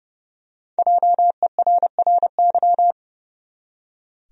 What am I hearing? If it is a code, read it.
Morse code, heard as a keyed tone, about 30 wpm, 700 Hz. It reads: JERRY
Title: USA Common Names - Max 5 Letters - 30wpm